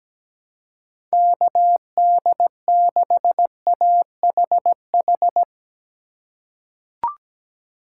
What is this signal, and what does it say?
Morse code, heard as a keyed tone, about 17 wpm, 700 Hz. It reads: KD6AHH
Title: USA Call Signs - 3x Character Spacing - 17wpm